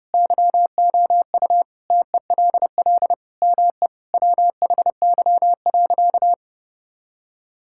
Morse code, heard as a keyed tone, about 30 wpm, 700 Hz. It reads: YOU TELL ME WHY.
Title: Sentences from Top 1000 Words - Review Part 6 - 30wpm